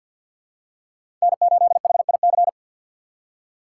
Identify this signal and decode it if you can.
Morse code, heard as a keyed tone, about 51 wpm, 700 Hz. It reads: N8HIC
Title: USA Call Signs - 5 Characters - 2x Character Spacing - 50wpm